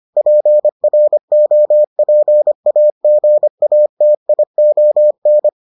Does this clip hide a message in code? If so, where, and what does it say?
Yes: Morse code from 0.2 to 5.5 s, heard as a keyed tone, about 25 wpm, 600 Hz. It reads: PROPAGATION